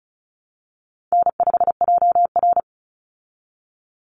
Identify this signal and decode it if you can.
Morse code, heard as a keyed tone, about 34 wpm, 700 Hz. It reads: N5JR
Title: Call Signs Repeated 3x - 35wpm